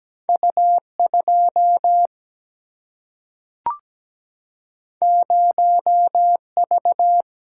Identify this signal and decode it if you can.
Morse code, heard as a keyed tone, about 17 wpm, 700 Hz. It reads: U 2 0 V
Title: Single-Letter Single-Number 17wpm